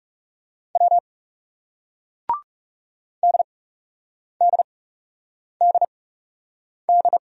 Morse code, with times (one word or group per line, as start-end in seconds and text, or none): 0.8-1.0 W
3.2-7.2 DDDD